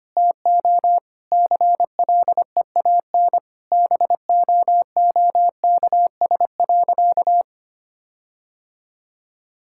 Morse code, heard as a keyed tone, about 25 wpm, 700 Hz. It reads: TO CLEAN BOOKS.
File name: practice / 25wpm / Sentences with 4-Word Phrases - 25wpm